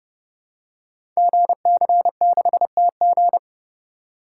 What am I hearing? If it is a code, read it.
Morse code, heard as a keyed tone, about 29 wpm, 700 Hz. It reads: GC6TG
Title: Code Groups - 5 Letter:Numbers - 30wpm